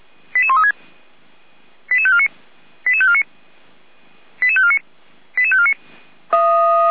Five-tone selective call (ZVEI), heard as quick runs of five tones, tones of about 70 ms. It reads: 88136 88549 88549 88549 88549
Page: SDR/ZVEI